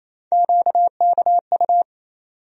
Morse code, heard as a keyed tone, about 28 wpm, 700 Hz. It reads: QKU
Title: USA Call Sign Postfixes - 3 Letters 28wpm